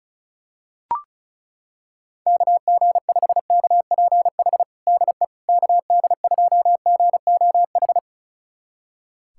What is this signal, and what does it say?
Morse code, heard as a keyed tone, about 36 wpm, 700 Hz. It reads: KG5KPH DE KD2GOH
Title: USA Callsign DE Callsign 35wpm